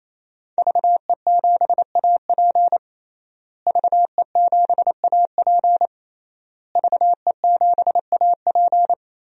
Morse code, heard as a keyed tone, about 28 wpm, 700 Hz. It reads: VE7AP VE7AP VE7AP
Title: DXCC-100 Call Signs - 5 Characters - Repeated 3x - 28wpm